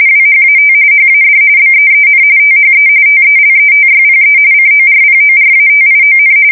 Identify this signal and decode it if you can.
RTTY, about 45.45 baud. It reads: WHAT HAD BEEN THE TELETYPE CORPORATION
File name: fsk170 / teletype-corp